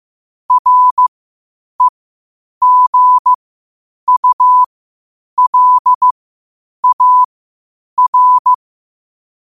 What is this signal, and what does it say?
Morse code, heard as a keyed tone, about 15 wpm, spaced as for 8 wpm, 1000 Hz. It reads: REGULAR
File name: qotdfiles / 8wpm